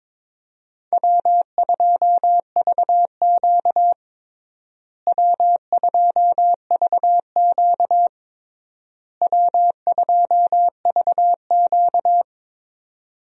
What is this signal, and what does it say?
Morse code, heard as a keyed tone, about 22 wpm, 700 Hz. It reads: W2VQ W2VQ W2VQ
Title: USA Call Signs - 4 Characters - Repeated 3x - 22wpm